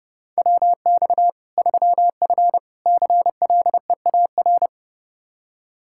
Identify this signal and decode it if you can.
Morse code, heard as a keyed tone, about 32 wpm, 700 Hz. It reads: WX 3F CLEAR
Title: K1USN SST QSO Practice 30wpm